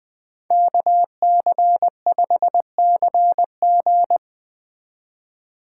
Morse code, heard as a keyed tone, about 20 wpm, 700 Hz. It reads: KC5CG
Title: Speed Racing - USA 2x2 Call Signs 17wpm